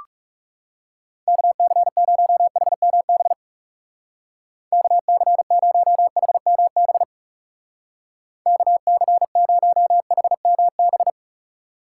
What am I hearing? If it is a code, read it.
Morse code, heard as a keyed tone, about 39 wpm, 700 Hz. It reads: KC0HMB KC0HMB KC0HMB